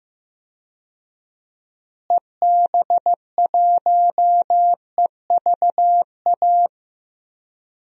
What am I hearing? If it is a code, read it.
Morse code, heard as a keyed tone, about 15 wpm, 700 Hz. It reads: EB1EVA
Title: DXCC-100 Call Signs - 6 Characters - Repeated 2x - 15wpm